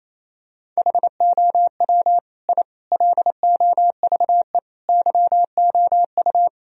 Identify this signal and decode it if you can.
Morse code, heard as a keyed tone, about 28 wpm, 700 Hz. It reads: HOW I LOVE YOU
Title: Sentences from Top 1000 Words - Review Part 1 - 28wpm